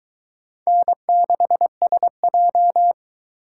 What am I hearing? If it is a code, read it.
Morse code, heard as a keyed tone, about 22 wpm, 700 Hz. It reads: N6SJ